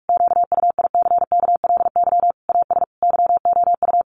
Morse code, heard as a keyed tone, about 44 wpm, 700 Hz. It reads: QUICKLY AS YOU
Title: Sentences from Top 1100 Words 45wpm